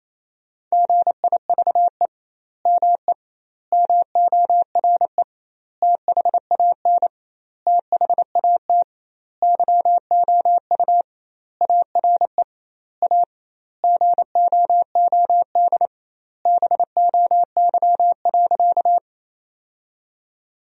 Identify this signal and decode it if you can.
Morse code, heard as a keyed tone, about 28 wpm, 700 Hz. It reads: GIVE ME MORE THAN THAT YOU ARE A GOOD BOY.
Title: 2x Word-Spacing - Sentences from Top 200 Words - 28wpm